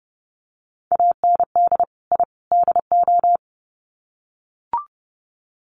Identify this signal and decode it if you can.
Morse code, heard as a keyed tone, about 30 wpm, 700 Hz. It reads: AND I DO
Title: Sets of 3 Words 30wpm